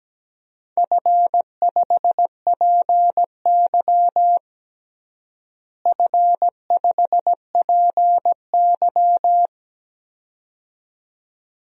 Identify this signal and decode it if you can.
Morse code, heard as a keyed tone, about 17 wpm, 700 Hz. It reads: F5PY F5PY
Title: DXCC-100 Call Signs - 4 Characters - Repeated 2x - 17wpm